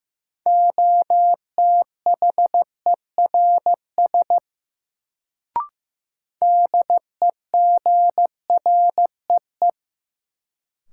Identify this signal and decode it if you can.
Morse code, heard as a keyed tone, about 15 wpm, 700 Hz. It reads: OTHERS DEGREE
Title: Sentences from Top 1500 Words - New Word Drill Alt - 15wpm